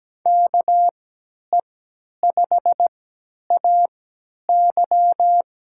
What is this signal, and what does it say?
Morse code, heard as a keyed tone, about 17 wpm, spaced as for 10 wpm, 700 Hz. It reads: KE5AY
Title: USA Call Signs - 3x Character Spacing - 17wpm